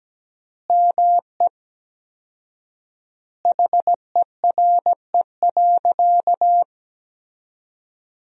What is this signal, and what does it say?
Morse code, heard as a keyed tone, about 17 wpm, 700 Hz. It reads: ME HERE.
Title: 4x Word-Spacing - Sentences from Top 200 Words - 17wpm